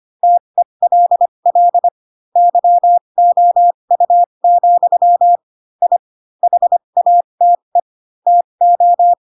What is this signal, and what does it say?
Morse code, heard as a keyed tone, about 25 wpm, spaced as for 22 wpm, 700 Hz. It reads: TELL YOU, I HATE TO
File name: Pinocchio / Pinocchio ch4 18 25 ik1hge